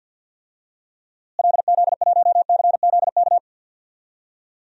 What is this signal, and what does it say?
Morse code, heard as a keyed tone, about 53 wpm, 700 Hz. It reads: LZ1CZK
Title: DXCC-100 Call Signs - 6 Characters - Repeated 3x - 50wpm